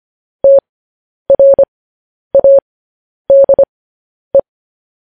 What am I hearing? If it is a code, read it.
Morse code, heard as a keyed tone, about 25 wpm, spaced as for 10 wpm, 550 Hz. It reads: TRADE
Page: news headlines in morse code at 10 wpm